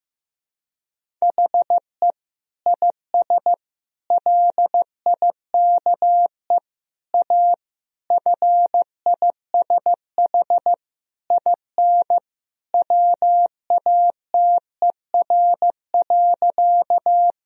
Morse code, heard as a keyed tone, about 15 wpm, 700 Hz. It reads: HE IS LIKE A FISH IN WATER.
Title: Sentences from Top 1000 Words - Review Part 1 - 15wpm